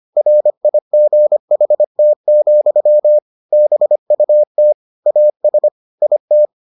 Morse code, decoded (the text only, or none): RIGHT, BUT AS IT